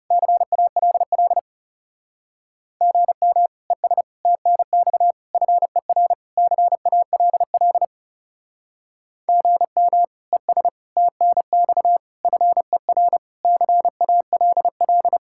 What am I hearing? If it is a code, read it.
Morse code, heard as a keyed tone, about 40 wpm, 700 Hz. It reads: CALL GM ES TNX FER CALL GM ES TNX FER CALL